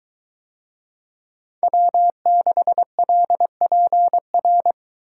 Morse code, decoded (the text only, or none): W6LPR